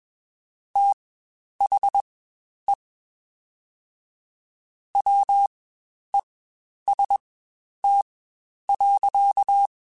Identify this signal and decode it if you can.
Morse code, heard as a keyed tone, about 21 wpm, spaced as for 10 wpm, 800 Hz. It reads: THE WEST.
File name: Spaced Characters / Spaced Characters-mp3s / Weather5 1 sp @20